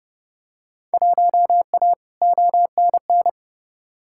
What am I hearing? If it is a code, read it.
Morse code, heard as a keyed tone, about 32 wpm, 700 Hz. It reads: 1A ONN
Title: Field Day Exchanges Encore - 30wpm